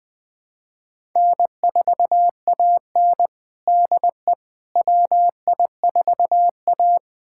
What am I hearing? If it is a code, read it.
Morse code, heard as a keyed tone, about 21 wpm, 700 Hz. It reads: N4AN DE WI4A
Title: USA Callsign DE Callsign - 4 Characters - 20wpm